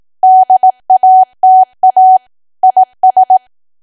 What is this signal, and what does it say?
Morse code, heard as a keyed tone, about 18 wpm, 750 Hz. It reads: DATA IS
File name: cwa / W1AW-Level2 / 18WPM